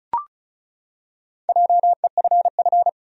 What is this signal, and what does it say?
Morse code, heard as a keyed tone, about 36 wpm, 700 Hz. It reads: JEFF
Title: USA Common Names - Max 5 Letters - 35wpm